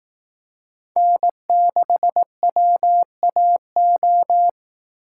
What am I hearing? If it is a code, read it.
Morse code, heard as a keyed tone, about 18 wpm, 700 Hz. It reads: N6WAO
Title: Speed Racing - USA Call Signs 15wpm